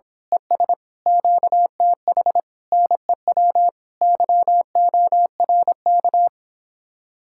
Morse code, heard as a keyed tone, about 26 wpm, 700 Hz. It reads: ES QTH NEW YORK
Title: Speed-Racing Common QSO Phrases Encore - 22wpm